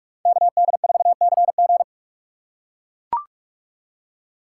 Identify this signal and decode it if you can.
Morse code, heard as a keyed tone, about 43 wpm, 700 Hz. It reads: KD4CG